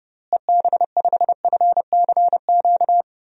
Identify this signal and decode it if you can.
Morse code, heard as a keyed tone, about 31 wpm, 700 Hz. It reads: EB5FCQ
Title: DXCC-100 Call Signs - 6 Characters - Repeated 2x - 30wpm